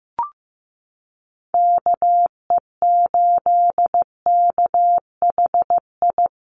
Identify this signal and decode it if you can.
Morse code, heard as a keyed tone, about 15 wpm, 700 Hz. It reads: KE8KHI
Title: USA Call Signs Repeated 2x 15wpm